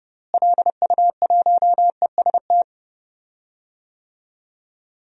Morse code, heard as a keyed tone, about 30 wpm, 700 Hz. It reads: LU1EST
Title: DXCC-100 Call Signs - 6 Characters - Repeated 3x - 30wpm